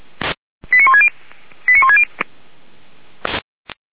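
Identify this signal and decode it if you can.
Five-tone selective call (ZVEI), heard as quick runs of five tones, tones of about 70 ms. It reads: 89160 89160
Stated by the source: Five-tone ZVEI selective call received on an SDR.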